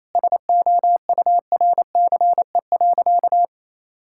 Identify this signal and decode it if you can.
Morse code, heard as a keyed tone, about 28 wpm, 700 Hz. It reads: SOURCE.
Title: Sentences from Top 1000 Words - Review Part 5 - 28wpm